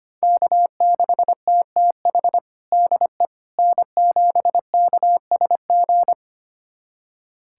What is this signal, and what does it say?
Morse code, heard as a keyed tone, about 25 wpm, 700 Hz. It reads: K6TTH DE N7KSG
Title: USA Callsign DE Callsign - 5 Characters - 25wpm